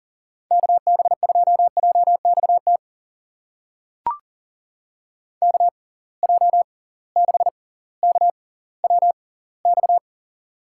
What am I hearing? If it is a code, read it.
Morse code, heard as a keyed tone, about 40 wpm, spaced as for 20 wpm, 700 Hz. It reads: KB2JXT KJ6KWX
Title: USA Call Signs - 6 Characters - 6x Character Spacing - 40wpm